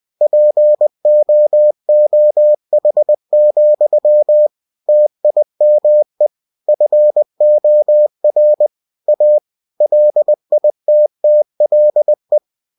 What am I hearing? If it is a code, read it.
Morse code, heard as a keyed tone, about 20 wpm, 600 Hz. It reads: POOH, TIME FOR A LITTLE
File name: cw-books / winnie-the-pooh / winnie-20wpm-page-0049